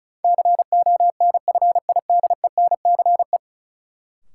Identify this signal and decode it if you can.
Morse code, heard as a keyed tone, about 34 wpm, 700 Hz. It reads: CONFIDENCE
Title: Sentences from Top 1500 Words - New Word Drill Alt - 35wpm